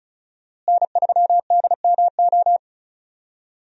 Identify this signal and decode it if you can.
Morse code, heard as a keyed tone, about 34 wpm, 700 Hz. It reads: N3DMO
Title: USA Call Signs - 5 Characters - Repeated 3x - 35wpm